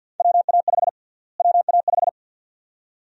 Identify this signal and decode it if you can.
Morse code, heard as a keyed tone, about 53 wpm, 700 Hz. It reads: WA5 WA5
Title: USA Call Sign Prefixes - Repeated 2x - 50wpm